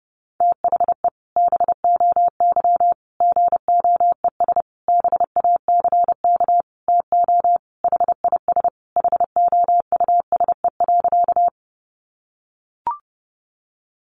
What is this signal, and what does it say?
Morse code, heard as a keyed tone, about 30 wpm, 700 Hz. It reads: THE BOY GOES BACK TO HIS HOUSE.
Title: Easy Phrases 30wpm